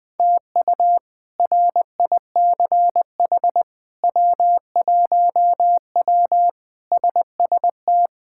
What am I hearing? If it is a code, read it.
Morse code, heard as a keyed tone, about 20 wpm, 700 Hz. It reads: TU RICH W1W SST
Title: K1USN SST QSO Practice - 20wpm